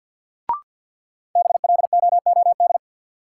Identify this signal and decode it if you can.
Morse code, heard as a keyed tone, about 53 wpm, 700 Hz. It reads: BLOOD
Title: Sentences from Top 1000 Words - Review Part 2 - New Word Drill - 50wpm